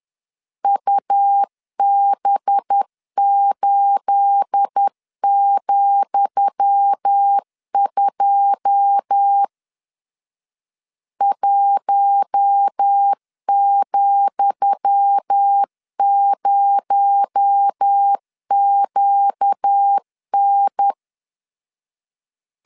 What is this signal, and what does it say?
Morse code, heard as a keyed tone, about 10 wpm, 800 Hz. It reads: UB8,2 1,0QN
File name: LEZIONI / 10 SIGNS / 10D COMA 10wpm 3013